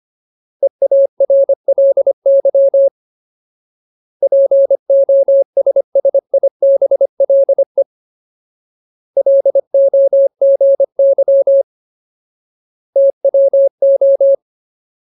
Morse code, heard as a keyed tone, about 25 wpm, 550 Hz. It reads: EARLY POSSIBLE LOGY TWO
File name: Ultimate Word List / 25WPM Ultimate